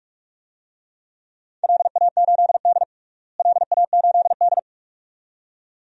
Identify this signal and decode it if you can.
Morse code, heard as a keyed tone, about 46 wpm, 700 Hz. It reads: LA8D LA8D
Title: DXCC-100 Call Signs - 4 Characters - Repeated 2x - 45wpm